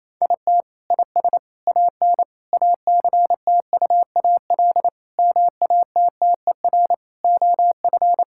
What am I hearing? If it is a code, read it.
Morse code, heard as a keyed tone, about 28 wpm, 700 Hz. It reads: IT IS AN ACTUAL MATTER OF
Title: Sentences from Top 1000 Words - Review Part 5 - 28wpm